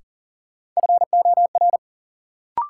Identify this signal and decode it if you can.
Morse code, heard as a keyed tone, about 40 wpm, 700 Hz. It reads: FOR
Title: Sentences from Top 100 Words - New Word Drill - 40wpm